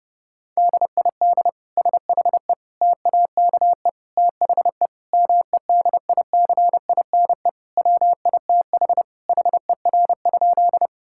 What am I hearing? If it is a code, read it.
Morse code, heard as a keyed tone, about 30 wpm, 700 Hz. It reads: DID SHE TAKE THE MEDICINE WITH HER?
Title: Sentences from Top 1000 Words - Review Part 6 - 30wpm